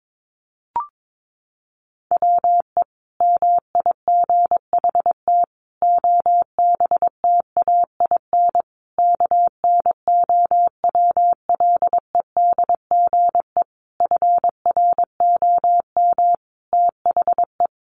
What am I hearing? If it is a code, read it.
Morse code, heard as a keyed tone, about 22 wpm, 700 Hz. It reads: WE MIGHT OBTAIN KNOWLEDGE FROM THE
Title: Sentences from Top 1300 Words 22wpm